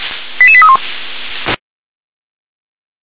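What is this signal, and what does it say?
Five-tone selective call (ZVEI), heard as quick runs of five tones, tones of about 70 ms. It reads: 88831
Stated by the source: Received on an SDR; ZVEI five-tone call.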